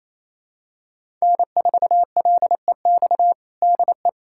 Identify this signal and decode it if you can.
Morse code, heard as a keyed tone, about 28 wpm, 700 Hz. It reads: N4LEX DE
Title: USA Callsign DE Callsign - 5 Characters - 28wpm